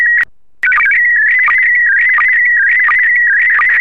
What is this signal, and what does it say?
SSTV (slow-scan television), PD 90.
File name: sstv / PD90